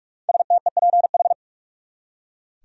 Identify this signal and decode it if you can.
Morse code, heard as a keyed tone, about 46 wpm, 700 Hz. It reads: STEPH
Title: USA Common Names - 45wpm